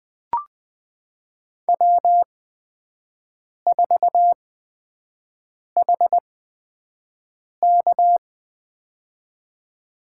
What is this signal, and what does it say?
Morse code, heard as a keyed tone, about 20 wpm, spaced as for 5 wpm, 700 Hz. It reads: W4HK
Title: USA Call Signs - 4 Characters - 8x Character Spacing - 20wpm